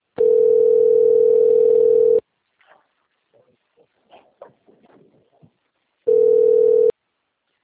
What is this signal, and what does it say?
Ringback tone (North America), 440 + 480 Hz.
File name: rg-1600-unknown- - - .3391